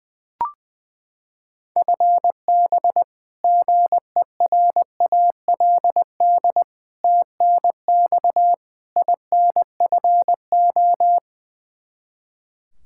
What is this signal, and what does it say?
Morse code, heard as a keyed tone, about 20 wpm, 700 Hz. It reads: FB GERALD TNX INFO
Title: Practice QSO - Exchange 3 - More Info - 20wpm